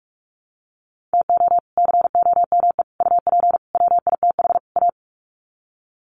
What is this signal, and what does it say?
Morse code, heard as a keyed tone, about 44 wpm, 700 Hz. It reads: TO COME UP WITH A
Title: Sets of 5 Words 45wpm